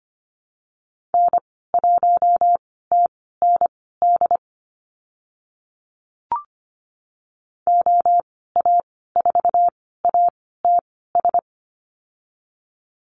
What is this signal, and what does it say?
Morse code, heard as a keyed tone, about 25 wpm, spaced as for 16 wpm, 690 Hz. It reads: N1TND OA4ATS
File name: Lesson 15g - Number 5 - Call Signs 25wpm